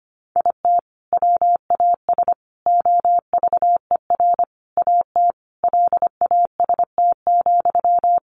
Morse code, heard as a keyed tone, about 25 wpm, 700 Hz. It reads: IT WAS OVER AT LAST,
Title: Sentences from Top 600 Words 25wpm